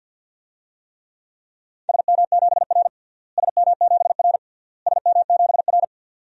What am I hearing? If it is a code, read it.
Morse code, heard as a keyed tone, about 50 wpm, 700 Hz. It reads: SM7R SM7R SM7R
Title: DXCC-100 Call Signs - 4 Characters - Repeated 3x - 50wpm